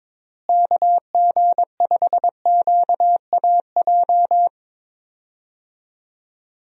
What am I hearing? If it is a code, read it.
Morse code, heard as a keyed tone, about 21 wpm, 700 Hz. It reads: KG5QAJ
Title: USA Call Signs - 6 Characters - Repeated 2x - 22wpm